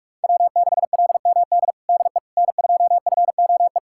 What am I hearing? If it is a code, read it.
Morse code, heard as a keyed tone, about 46 wpm, 700 Hz. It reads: W6LMD DE N2FOE
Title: USA Callsign DE Callsign - 5 Characters - 45wpm